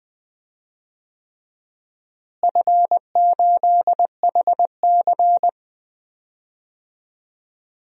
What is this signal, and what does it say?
Morse code, heard as a keyed tone, about 19 wpm, 700 Hz. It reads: F8HC
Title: DXCC-100 Call Signs - 4 Characters - Repeated 3x - 20wpm